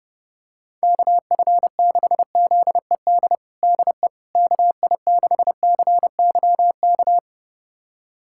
Morse code, heard as a keyed tone, about 30 wpm, 700 Hz. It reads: KF6ZED DE KI6CYK
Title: USA Callsign DE Callsign - 6 Characters - 30wpm